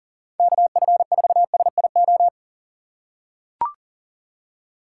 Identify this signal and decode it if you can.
Morse code, heard as a keyed tone, about 37 wpm, 700 Hz. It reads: KF4SIO
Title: USA Call Signs - 6 Characters - 3x Character Spacing - 40wpm